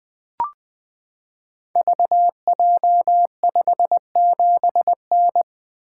Morse code, heard as a keyed tone, about 20 wpm, 700 Hz. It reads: VJ57N
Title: Code Groups - 5 Letter:Numbers - 20wpm